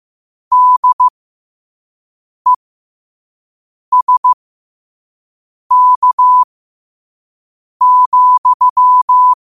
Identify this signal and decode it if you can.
Morse code, heard as a keyed tone, about 15 wpm, spaced as for 5 wpm, 1000 Hz. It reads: DESK,
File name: qotdfiles / 5wpm